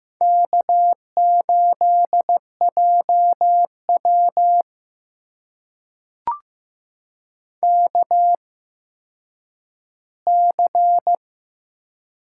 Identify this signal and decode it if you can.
Morse code, heard as a keyed tone, about 15 wpm, spaced as for 9 wpm, 700 Hz. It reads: K8JW KC
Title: USA Call Signs - 8x Character Spacing - 15wpm